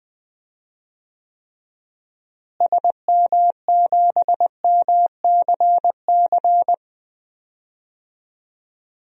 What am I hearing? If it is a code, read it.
Morse code, heard as a keyed tone, about 20 wpm, 700 Hz. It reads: SM7MCC
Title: DXCC-100 Call Signs - 6 Characters - Repeated 2x - 20wpm